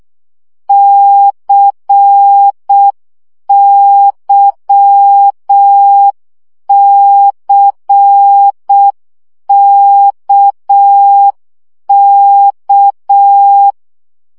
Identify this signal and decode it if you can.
Morse code, heard as a keyed tone, about 6 wpm, 800 Hz. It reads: CYCKK
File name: CW gyakorlas k c y 5 baud